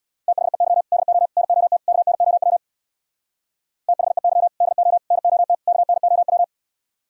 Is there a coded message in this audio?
Yes: Morse code from 0.3 to 6.4 s, heard as a keyed tone, about 64 wpm, 700 Hz. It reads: THEY DO NOT KNOW THEY DO NOT KNOW